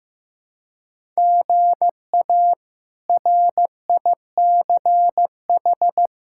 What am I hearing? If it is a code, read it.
Morse code, heard as a keyed tone, about 16 wpm, 700 Hz. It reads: GA RICH
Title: K1USN SST QSO Practice - 15wpm